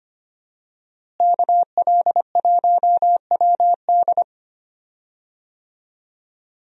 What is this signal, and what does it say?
Morse code, heard as a keyed tone, about 25 wpm, 700 Hz. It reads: KL1WD
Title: USA Call Signs - 5 Characters - Repeated 2x - 25wpm